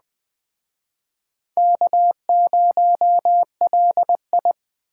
Morse code, heard as a keyed tone, about 20 wpm, 700 Hz. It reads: K0LI